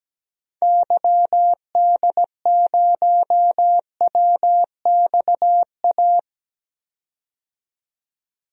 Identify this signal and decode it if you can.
Morse code, heard as a keyed tone, about 17 wpm, 700 Hz. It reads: YD0WXA
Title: DXCC-100 Call Signs - 6 Characters - 17wpm